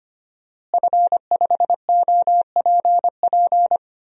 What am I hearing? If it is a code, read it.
Morse code, heard as a keyed tone, about 25 wpm, 700 Hz. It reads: F5OPP
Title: DXCC-100 Call Signs - 5 Characters - Repeated 2x - 25wpm